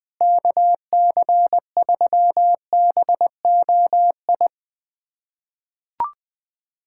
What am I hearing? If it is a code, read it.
Morse code, heard as a keyed tone, about 20 wpm, 700 Hz. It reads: KC3BOI
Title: USA Callsign DE Callsign 20wpm